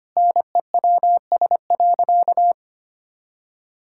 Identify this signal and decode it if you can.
Morse code, heard as a keyed tone, about 25 wpm, 700 Hz. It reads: NEWS.
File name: practice / 25wpm / Sentences with 2-Word Phrases - 25wpm